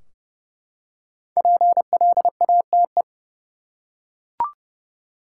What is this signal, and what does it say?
Morse code, heard as a keyed tone, about 30 wpm, 700 Hz. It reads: PLATE
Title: Five Letter Words - 30wpm